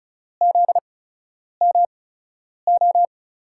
Morse code, heard as a keyed tone, about 36 wpm, spaced as for 10 wpm, 700 Hz. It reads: ZMO
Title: USA Call Signs - 6 Characters - 8x Character Spacing - 35wpm